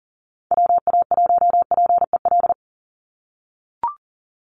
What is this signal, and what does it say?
Morse code, heard as a keyed tone, about 40 wpm, 700 Hz. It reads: WA1PEL
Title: USA Call Signs 40wpm